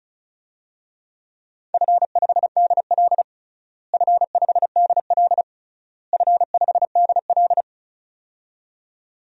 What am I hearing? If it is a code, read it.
Morse code, heard as a keyed tone, about 34 wpm, 700 Hz. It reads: F5DL F5DL F5DL
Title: DXCC-100 Call Signs - 4 Characters - Repeated 3x - 35wpm